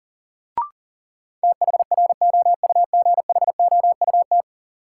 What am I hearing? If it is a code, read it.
Morse code, heard as a keyed tone, about 38 wpm, 700 Hz. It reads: THROUGHOUT
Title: Sentences from Top 800 Words - New Word Drill - 40wpm